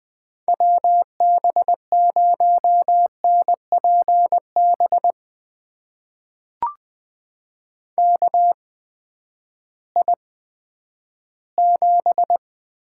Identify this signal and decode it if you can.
Morse code, heard as a keyed tone, about 20 wpm, spaced as for 11 wpm, 700 Hz. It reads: WB0NPB KI7
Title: USA Call Signs - 6 Characters - 8x Character Spacing - 20wpm